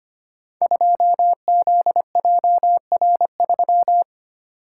Morse code, heard as a keyed tone, about 25 wpm, 700 Hz. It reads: 2ZJR3
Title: Code Groups - 5 Letter:Numbers - 25wpm